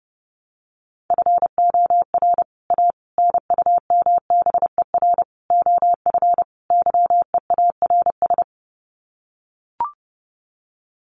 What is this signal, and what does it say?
Morse code, heard as a keyed tone, about 30 wpm, 700 Hz. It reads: FOR A NUMBER OF YEARS
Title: Sets of 5 Words Encore 30wpm